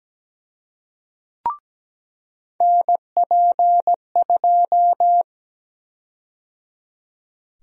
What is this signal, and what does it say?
Morse code, heard as a keyed tone, about 17 wpm, 700 Hz. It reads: NP2
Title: USA Call Sign Prefixes - 17wpm